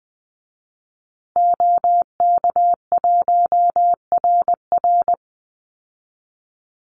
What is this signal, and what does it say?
Morse code, heard as a keyed tone, about 20 wpm, 700 Hz. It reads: OK1RR
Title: Call Signs Repeated 2x - 20wpm